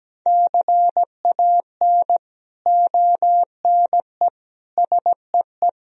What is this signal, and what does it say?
Morse code, heard as a keyed tone, about 17 wpm, 700 Hz. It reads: CAN ONE SEE
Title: Sentences from Top 100 Words and Abbreviations 17wpm